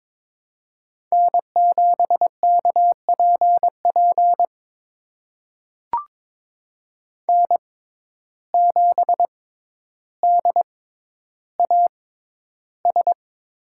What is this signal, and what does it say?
Morse code, heard as a keyed tone, about 22 wpm, spaced as for 11 wpm, 700 Hz. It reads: N7KPP N7DAS
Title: USA Call Signs - 5 Characters - 6x Character Spacing - 22wpm